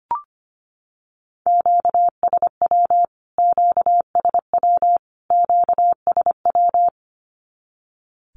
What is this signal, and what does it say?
Morse code, heard as a keyed tone, about 25 wpm, 700 Hz. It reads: QSW QSW QSW
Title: Q-Codes Repeated 3x 25wpm